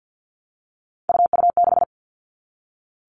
Morse code, heard as a keyed tone, about 50 wpm, 700 Hz. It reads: UU6
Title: All Cut Numbers - 3-Digit 50wpm